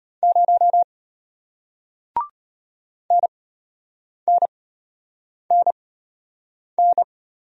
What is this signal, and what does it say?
Morse code, heard as a keyed tone, about 33 wpm, spaced as for 8 wpm, 700 Hz. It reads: 0 NNNN